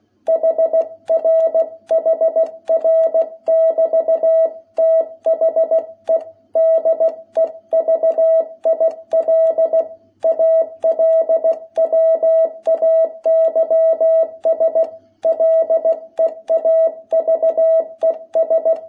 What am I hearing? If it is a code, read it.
Morse code, heard as a keyed tone, about 16 wpm, 660 Hz. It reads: HR HR<BT>THE DEVIL ALWAYS LEAVES
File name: jh8cbh / 2021cw